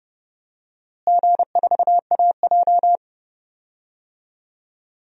Morse code, heard as a keyed tone, about 30 wpm, 700 Hz. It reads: G4AJ
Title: DXCC-100 Call Signs - 4 Characters - Repeated 2x - 30wpm